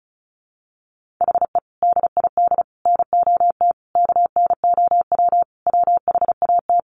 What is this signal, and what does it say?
Morse code, heard as a keyed tone, about 34 wpm, 700 Hz. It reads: HE DID NOT KNOW WHAT
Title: Sentences from Top 600 Words 35wpm